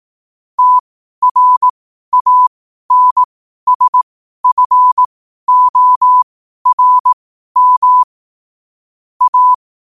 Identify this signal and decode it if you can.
Morse code, heard as a keyed tone, about 18 wpm, spaced as for 12 wpm, 1000 Hz. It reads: TRANSFORM A
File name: qotdfiles / 12wpm